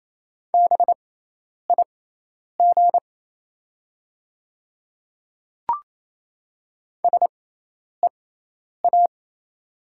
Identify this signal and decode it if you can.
Morse code, heard as a keyed tone, about 27 wpm, spaced as for 10 wpm, 700 Hz. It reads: BIG SEA
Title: 6x Character Spacing - Top 300 Words - 28wpm